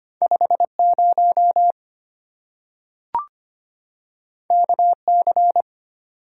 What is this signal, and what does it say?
Morse code, heard as a keyed tone, about 25 wpm, 700 Hz. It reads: 50 KC
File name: Two Letter-Number Combinations - 25wpm